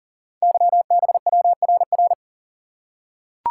Morse code, heard as a keyed tone, about 38 wpm, 700 Hz. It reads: YBWRR
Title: Code Groups - 5 Letters - 40wpm